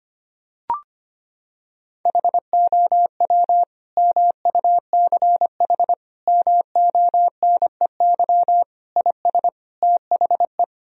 Morse code, heard as a keyed tone, about 25 wpm, 700 Hz. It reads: HOW MUCH MONEY IS THE
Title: Sentences from Top 1000 Words - Review Part 4 - 25wpm